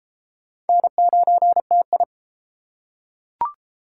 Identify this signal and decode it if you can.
Morse code, heard as a keyed tone, about 31 wpm, 700 Hz. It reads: N9TI